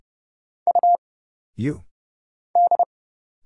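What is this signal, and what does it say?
Morse code, heard as a keyed tone, about 29 wpm, 700 Hz. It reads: U D
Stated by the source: Morse Code Ninja single-character practice (letters, numbers, punctuation, prosigns) at 30 wpm, by Kurt Zoglmann (AD0WE).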